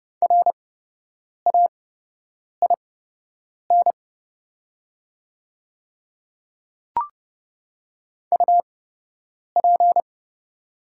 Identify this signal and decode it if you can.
Morse code, heard as a keyed tone, about 30 wpm, spaced as for 8 wpm, 690 Hz. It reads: RAIN UP
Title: 8x Character Spacing - Top 300 Words - 30wpm